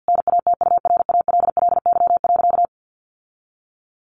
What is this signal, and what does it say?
Morse code, heard as a keyed tone, about 49 wpm, 700 Hz. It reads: NATURALLY.
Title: Sentences from Top 1300 Words 50wpm